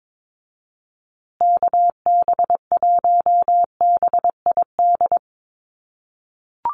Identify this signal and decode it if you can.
Morse code, heard as a keyed tone, about 22 wpm, 700 Hz. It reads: KB1BID
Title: USA Call Signs Repeated 2x 22wpm